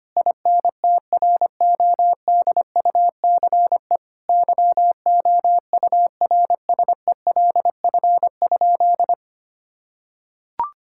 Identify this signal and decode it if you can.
Morse code, heard as a keyed tone, about 25 wpm, 700 Hz. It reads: INTRODUCE YOURSELF?
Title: Sentences from Top 1600 Words 25wpm